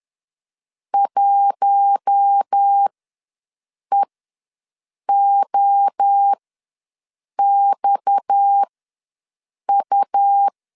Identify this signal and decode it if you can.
Morse code, heard as a keyed tone, about 10 wpm, spaced as for 6 wpm, 800 Hz. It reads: 1EOXU
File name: LEZIONI / 6 PKX0 / 6 PKX ws 10wpm 3706